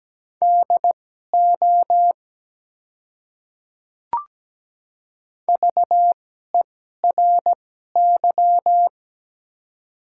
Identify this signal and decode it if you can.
Morse code, heard as a keyed tone, about 17 wpm, spaced as for 12 wpm, 690 Hz. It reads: DO VERY
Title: 2x Character Spacing - Top 300 Words - 17wpm